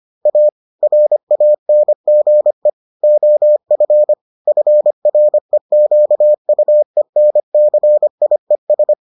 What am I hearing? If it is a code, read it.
Morse code, heard as a keyed tone, about 25 wpm, 600 Hz. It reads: A RANGE OF FREQUENCIES